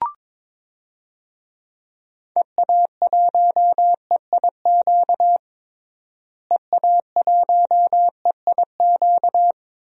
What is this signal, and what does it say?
Morse code, heard as a keyed tone, about 22 wpm, 700 Hz. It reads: EA1EIQ EA1EIQ
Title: DXCC-100 Call Signs - 6 Characters - Repeated 3x - 22wpm